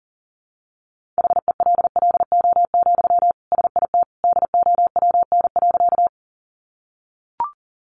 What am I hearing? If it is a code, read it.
Morse code, heard as a keyed tone, about 37 wpm, 700 Hz. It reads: HELLO, SIT DOWN.
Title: Sentences from Top 1100 Words 40wpm